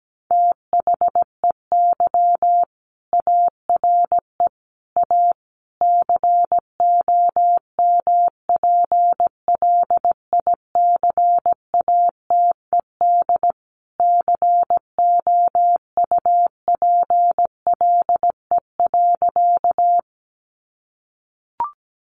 Morse code, heard as a keyed tone, about 17 wpm, 700 Hz. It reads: THEY ARE A COMPLICATED COUPLE.
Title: Sentences from Top 1200 Words 17wpm